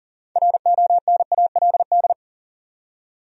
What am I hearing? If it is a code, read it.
Morse code, heard as a keyed tone, about 40 wpm, 700 Hz. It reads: RONALD
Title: USA Common Names - 40wpm